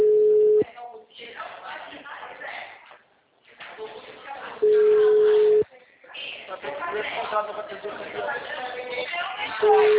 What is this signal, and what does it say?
Ringback tone (Europe, 425 Hz), with speech.